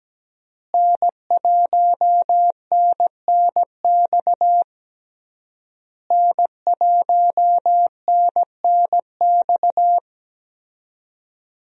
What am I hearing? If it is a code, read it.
Morse code, heard as a keyed tone, about 17 wpm, 700 Hz. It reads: N1NNX N1NNX
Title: USA Call Signs - 5 Characters - Repeated 3x - 17wpm